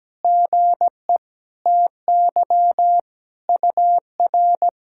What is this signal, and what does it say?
Morse code, heard as a keyed tone, about 17 wpm, 700 Hz. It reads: GE TY UR
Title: SOTA QSO Practice 17wpm